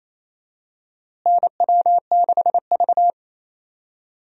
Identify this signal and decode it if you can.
Morse code, heard as a keyed tone, about 28 wpm, 700 Hz. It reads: NW6V
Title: USA Call Signs - 4 Characters - Repeated 3x - 28wpm